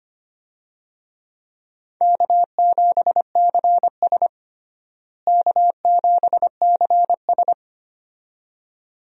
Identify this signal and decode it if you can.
Morse code, heard as a keyed tone, about 25 wpm, 700 Hz. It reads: K7CS K7CS
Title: USA Call Signs - 4 Characters - Repeated 2x - 25wpm